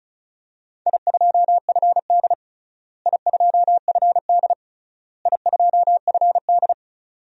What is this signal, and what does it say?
Morse code, heard as a keyed tone, about 36 wpm, 700 Hz. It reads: I2FD I2FD I2FD
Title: DXCC-100 Call Signs - 4 Characters - Repeated 3x - 35wpm